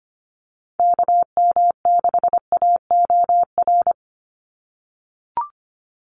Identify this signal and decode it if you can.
Morse code, heard as a keyed tone, about 25 wpm, 700 Hz. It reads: KM6AOR
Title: USA Call Signs 25wpm